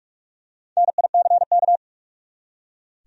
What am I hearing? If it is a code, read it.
Morse code, heard as a keyed tone, about 43 wpm, 700 Hz. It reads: NICK
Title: USA Common Names - 45wpm